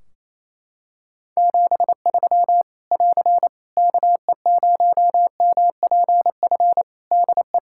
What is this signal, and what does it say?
Morse code, heard as a keyed tone, about 28 wpm, 700 Hz. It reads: 73 <AR> KE0MPF DE
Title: Practice QSO - Exchange 4 - Ending - 28wpm